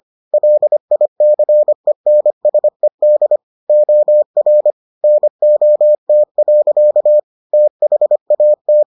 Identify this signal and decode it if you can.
Morse code, heard as a keyed tone, about 25 wpm, 600 Hz. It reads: LICENSED OR NOT. THAT